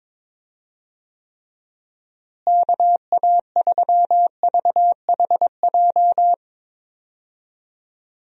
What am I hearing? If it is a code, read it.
Morse code, heard as a keyed tone, about 22 wpm, 700 Hz. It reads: KA3VHJ
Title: USA Call Signs - 6 Characters - Repeated 2x - 22wpm